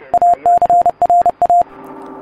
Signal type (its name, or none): Morse code